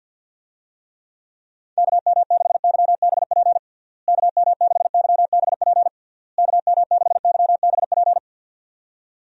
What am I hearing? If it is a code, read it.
Morse code, heard as a keyed tone, about 50 wpm, 700 Hz. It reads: KM6YBP KM6YBP KM6YBP
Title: USA Call Signs - 6 Characters - Repeated 3x - 50wpm